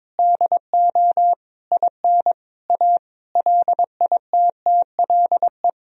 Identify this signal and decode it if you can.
Morse code, heard as a keyed tone, about 22 wpm, 700 Hz. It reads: DO IN A LITTLE
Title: Sentences from Top 1000 Words - Review Part 2 - 22wpm